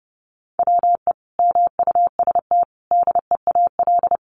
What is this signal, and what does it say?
Morse code, heard as a keyed tone, about 30 wpm, 700 Hz. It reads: WE MUST DEAL
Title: Sentences from Top 1400 Words 30wpm